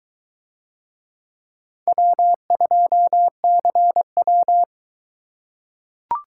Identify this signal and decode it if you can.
Morse code, heard as a keyed tone, about 21 wpm, 700 Hz. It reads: W2CW